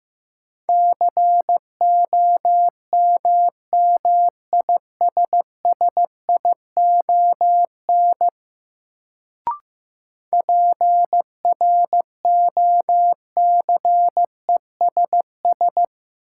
Morse code, heard as a keyed tone, about 15 wpm, 700 Hz. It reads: COMMISSION PROCESS
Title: Sentences from Top 1400 Words - New Word Drill Alt - 15wpm